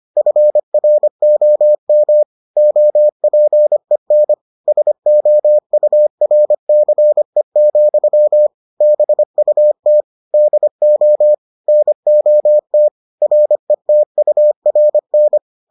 Morse code, heard as a keyed tone, about 25 wpm, 600 Hz. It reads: FROM OPEN SOURCE, BUT DO NOT RETURN